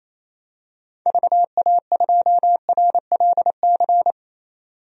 Morse code, heard as a keyed tone, about 27 wpm, 700 Hz. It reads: VA2RLC
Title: DXCC-100 Call Signs - 6 Characters - 28wpm